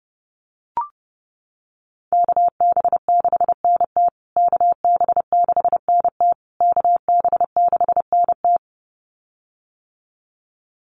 Morse code, heard as a keyed tone, about 30 wpm, 700 Hz. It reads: KB6NT KB6NT KB6NT
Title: USA Call Signs Repeated 3x 30wpm